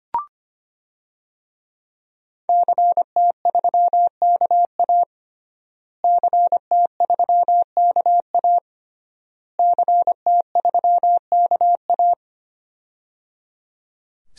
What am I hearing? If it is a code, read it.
Morse code, heard as a keyed tone, about 25 wpm, 700 Hz. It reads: CT3KA CT3KA CT3KA
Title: DXCC-100 Call Signs - 5 Characters - Repeated 3x - 25wpm